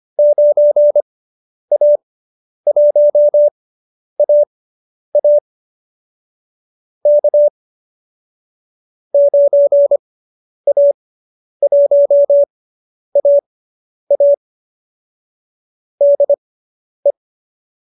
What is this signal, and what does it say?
Morse code, heard as a keyed tone, about 25 wpm, spaced as for 10 wpm, 590 Hz. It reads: 9A1AA K 9A1AA DE